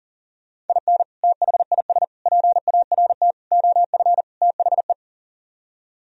Morse code, heard as a keyed tone, about 48 wpm, 700 Hz. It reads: IN THIS PART OF THE
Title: Speed-Racing - Sets of 5 Words 40wpm